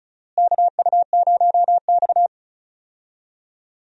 Morse code, heard as a keyed tone, about 34 wpm, 700 Hz. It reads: KU0X
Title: USA Call Signs - 4 Characters - Repeated 2x - 35wpm